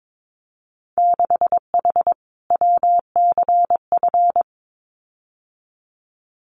Morse code, heard as a keyed tone, about 22 wpm, 700 Hz. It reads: 6H WCF
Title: Winter Field Day Exchanges Encore 22wpm